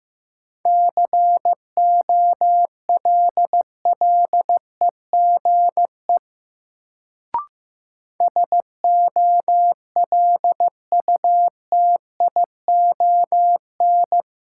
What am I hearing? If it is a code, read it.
Morse code, heard as a keyed tone, about 15 wpm, 700 Hz. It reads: COLLEGE SOLUTION
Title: Sentences from Top 1300 Words - New Word Drill - 15wpm